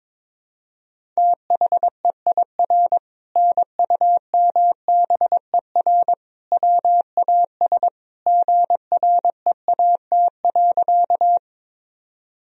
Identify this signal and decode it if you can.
Morse code, heard as a keyed tone, about 22 wpm, 700 Hz. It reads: THEIR NUMBER WAS GREAT.
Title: Sentences from Top 1000 Words - Review Part 1 - 22wpm